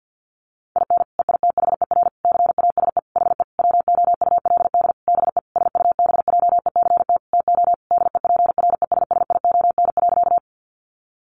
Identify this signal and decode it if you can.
Morse code, heard as a keyed tone, about 50 wpm, 700 Hz. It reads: IN EITHER CASE HE WOULD BE SUBJECT TO DEPRESSION.
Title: Sentences from Top 1300 Words 50wpm